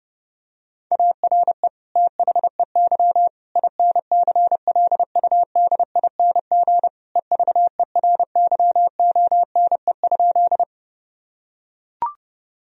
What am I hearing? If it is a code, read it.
Morse code, heard as a keyed tone, about 30 wpm, 700 Hz. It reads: ARE THEY INCLUDING EVERYONE?
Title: Sentences from Top 1000 Words - Review Part 4 - 30wpm